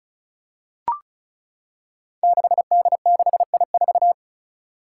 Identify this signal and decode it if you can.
Morse code, heard as a keyed tone, about 36 wpm, 700 Hz. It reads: 6D6I4
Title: Five Challenging Letter-Number Combinations 35wpm